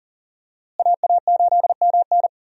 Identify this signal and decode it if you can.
Morse code, heard as a keyed tone, about 40 wpm, 700 Hz. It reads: AA8MN
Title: USA Call Signs - 5 Characters - Repeated 3x - 40wpm